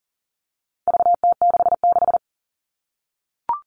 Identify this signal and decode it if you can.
Morse code, heard as a keyed tone, about 40 wpm, 700 Hz. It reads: VT66
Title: All Cut Numbers - 4-Digit 40wpm